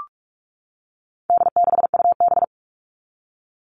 Morse code, heard as a keyed tone, about 44 wpm, 700 Hz. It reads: D6UB
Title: All Cut Numbers - 4-Digit 45wpm